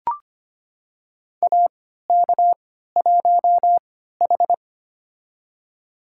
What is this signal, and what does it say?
Morse code, heard as a keyed tone, about 25 wpm, spaced as for 14 wpm, 700 Hz. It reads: AK1H
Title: USA Call Signs - 3x Character Spacing - 25wpm